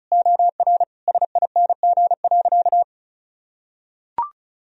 Morse code, heard as a keyed tone, about 34 wpm, 700 Hz. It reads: OR SING.
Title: Sentences from Top 1000 Words - Review Part 2 - 35wpm